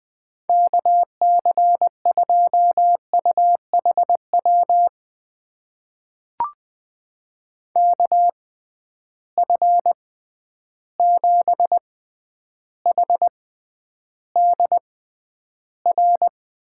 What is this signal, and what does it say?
Morse code, heard as a keyed tone, about 21 wpm, spaced as for 10 wpm, 700 Hz. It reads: KC2UHW KF7HDR
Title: USA Call Signs - 6 Characters - 6x Character Spacing - 20wpm